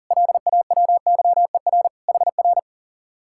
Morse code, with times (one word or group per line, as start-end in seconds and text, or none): 0.1-1.9 LAWYER
2.1-2.6 HR